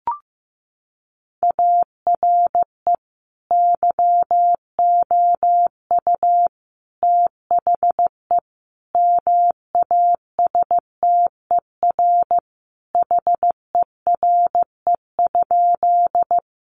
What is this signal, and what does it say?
Morse code, heard as a keyed tone, about 15 wpm, 700 Hz. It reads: ARE YOU THE MASTER HERE?
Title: Sentences from Top 1300 Words 15wpm